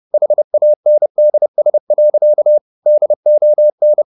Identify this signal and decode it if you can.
Morse code, heard as a keyed tone, about 30 wpm, 600 Hz. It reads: HANDS. DON